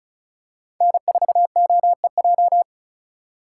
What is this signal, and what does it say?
Morse code, heard as a keyed tone, about 36 wpm, 700 Hz. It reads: N4OEJ
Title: USA Call Signs - 5 Characters - 3x Character Spacing - 35wpm